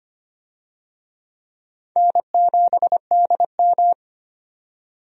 Morse code, heard as a keyed tone, about 25 wpm, 700 Hz. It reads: N7DM
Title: USA Call Signs - 4 Characters - 25wpm